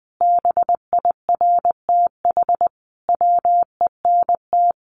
Morse code, heard as a keyed tone, about 20 wpm, 700 Hz. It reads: BIRTH WENT
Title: Sentences from Top 800 Words 20wpm